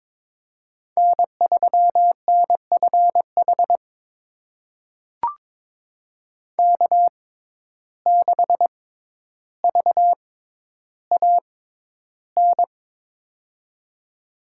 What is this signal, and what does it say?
Morse code, heard as a keyed tone, about 22 wpm, spaced as for 11 wpm, 700 Hz. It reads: N3NFH K6VAN
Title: USA Call Signs - 5 Characters - 6x Character Spacing - 22wpm